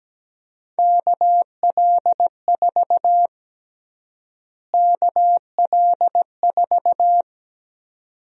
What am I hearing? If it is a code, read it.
Morse code, heard as a keyed tone, about 17 wpm, 700 Hz. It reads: KL4 KL4
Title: USA Call Sign Prefixes - Repeated 2x - 17wpm